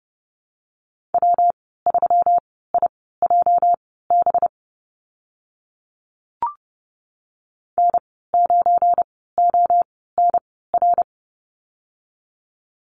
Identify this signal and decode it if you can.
Morse code, heard as a keyed tone, about 30 wpm, spaced as for 17 wpm, 700 Hz. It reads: W3IJB N9ONR
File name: Lesson 37g - Letter x - Call Signs 30wpm